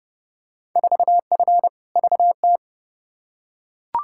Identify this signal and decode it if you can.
Morse code, heard as a keyed tone, about 31 wpm, 700 Hz. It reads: 4F VT
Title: Field Day Exchanges Encore - 30wpm